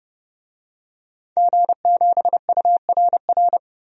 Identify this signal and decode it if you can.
Morse code, heard as a keyed tone, about 30 wpm, 700 Hz. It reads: G7URR
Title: DXCC-100 Call Signs - 5 Characters - Repeated 3x - 30wpm